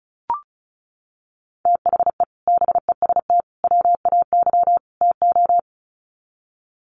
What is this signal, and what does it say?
Morse code, heard as a keyed tone, about 34 wpm, 700 Hz. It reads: THE BEST WAY TO
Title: Sets of 4 Words 35wpm